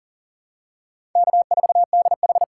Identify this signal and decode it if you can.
Morse code, heard as a keyed tone, about 40 wpm, 700 Hz. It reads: K4DH
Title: USA Call Signs - 4 Characters - 2x Character Spacing - 40wpm